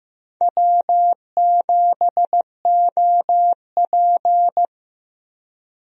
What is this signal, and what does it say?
Morse code, heard as a keyed tone, about 15 wpm, 700 Hz. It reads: W7OP
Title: USA Call Signs - 4 Characters - Repeated 3x - 15wpm